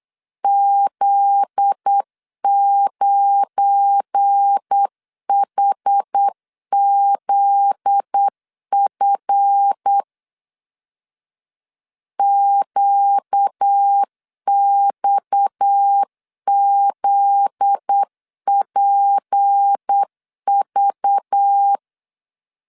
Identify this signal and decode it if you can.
Morse code, heard as a keyed tone, about 8 wpm, 800 Hz. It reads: Z9HZF QXZPV
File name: LEZIONI / 7 GF9Z / 7 GF9Z 8wpm 3213